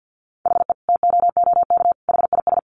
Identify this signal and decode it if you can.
Morse code, heard as a keyed tone, about 49 wpm, 700 Hz. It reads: HE TOOK HIS
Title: Sentences from Top 900 Words 50wpm